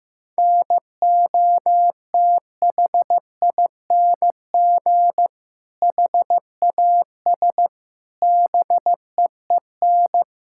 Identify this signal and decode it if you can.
Morse code, heard as a keyed tone, about 15 wpm, 700 Hz. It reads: NOTHING HAS BEEN
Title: Sentences from Top 1000 Words - Review Part 5 - 15wpm